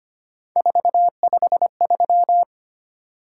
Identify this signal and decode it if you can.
Morse code, heard as a keyed tone, about 25 wpm, 700 Hz. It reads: 453
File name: Three Letter-Number Combinations - 25wpm